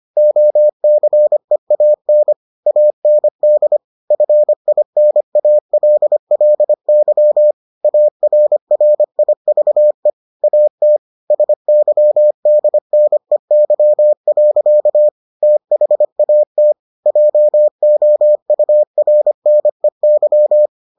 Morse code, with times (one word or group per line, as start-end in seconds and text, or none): 0.2-2.3 OCEAN
2.7-3.8 AND
4.1-7.5 FINALLY
7.8-10.1 ARRIVE
10.4-11.0 AT
11.3-15.1 SYDNEY.
15.4-16.7 THAT
17.1-20.7 JOURNEY